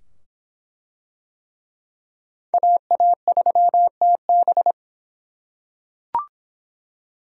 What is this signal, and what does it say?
Morse code, heard as a keyed tone, about 24 wpm, 700 Hz. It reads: AA3TB